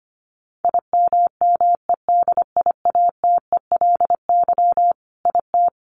Morse code, heard as a keyed tone, about 25 wpm, 700 Hz. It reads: IMMEDIATELY IT
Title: Sentences from Top 1100 Words 25wpm